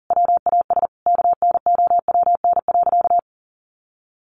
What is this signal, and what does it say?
Morse code, heard as a keyed tone, about 37 wpm, 700 Hz. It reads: WAS KNOWN.
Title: Sentences from Top 1400 Words 40wpm